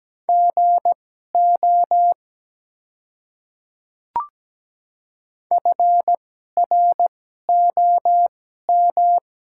Morse code, heard as a keyed tone, about 17 wpm, spaced as for 12 wpm, 700 Hz. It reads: GO FROM
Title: 2x Character Spacing - Top 100 Words - 17wpm